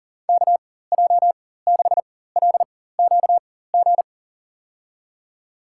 Morse code, heard as a keyed tone, about 40 wpm, spaced as for 19 wpm, 700 Hz. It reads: KJ6LQG
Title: USA Call Signs - 4x Character Spacing - 40wpm